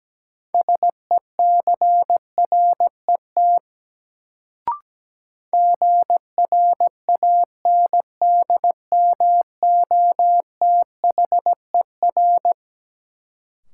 Morse code, heard as a keyed tone, about 17 wpm, 700 Hz. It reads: SECRET GRANDMOTHER
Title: Sentences from Top 1200 Words - New Word Drill Alt - 17wpm